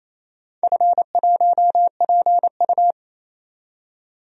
Morse code, heard as a keyed tone, about 27 wpm, 690 Hz. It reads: F1PU
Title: DXCC-100 Call Signs - 4 Characters - 28wpm